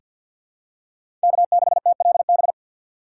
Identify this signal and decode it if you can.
Morse code, heard as a keyed tone, about 47 wpm, 700 Hz. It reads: K6TLB
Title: USA Call Signs - 5 Characters - 3x Character Spacing - 50wpm